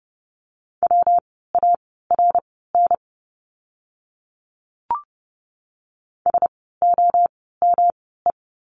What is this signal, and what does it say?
Morse code, heard as a keyed tone, about 30 wpm, spaced as for 17 wpm, 700 Hz. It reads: WARN SOME
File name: course / 30:10wpm / 115 Lesson 19f - Letter w - Words 30wpm